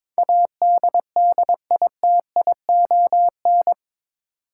Morse code, heard as a keyed tone, about 22 wpm, 700 Hz. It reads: ADDITION
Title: Sentences from Top 1500 Words - New Word Drill Alt - 22wpm